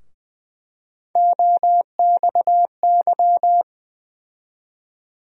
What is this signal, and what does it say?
Morse code, heard as a keyed tone, about 20 wpm, 700 Hz. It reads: OXY
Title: USA Call Sign Postfixes - 3 Letters 20wpm